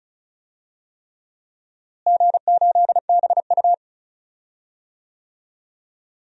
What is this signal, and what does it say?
Morse code, heard as a keyed tone, about 36 wpm, 700 Hz. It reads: G8BU
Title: DXCC-100 Call Signs - 4 Characters - Repeated 3x - 35wpm